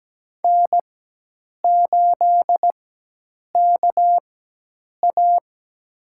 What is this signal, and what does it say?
Morse code, heard as a keyed tone, about 17 wpm, spaced as for 8 wpm, 700 Hz. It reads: N8KA
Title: USA Call Signs - 4x Character Spacing - 17wpm